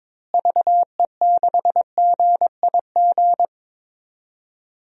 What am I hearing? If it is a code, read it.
Morse code, heard as a keyed tone, about 21 wpm, 700 Hz. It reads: VE6GIG
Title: DXCC-100 Call Signs - 6 Characters - Repeated 3x - 22wpm